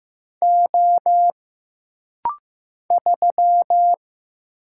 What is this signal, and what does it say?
Morse code, heard as a keyed tone, about 15 wpm, 700 Hz. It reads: O 3
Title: Single Letter-Number Alt - Rapid-Fire 15wpm